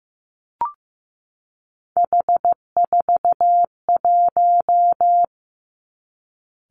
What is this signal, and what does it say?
Morse code, heard as a keyed tone, about 15 wpm, 700 Hz. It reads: H41
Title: Three Challenging Letter-Number Combinations 15wpm